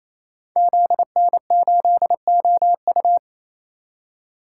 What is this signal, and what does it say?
Morse code, heard as a keyed tone, about 28 wpm, 700 Hz. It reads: ZN8OU
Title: Code Groups - 5 Letter:Numbers - 28wpm